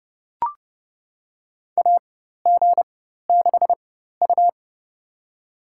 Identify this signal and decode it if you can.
Morse code, heard as a keyed tone, about 31 wpm, spaced as for 14 wpm, 700 Hz. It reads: AG6U
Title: USA Call Signs - 4 Characters - 4x Character Spacing - 30wpm